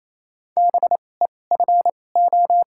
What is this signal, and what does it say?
Morse code, heard as a keyed tone, about 28 wpm, spaced as for 20 wpm, 700 Hz. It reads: BEFO
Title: 2x Character Spacing - Top 200 Words - 28wpm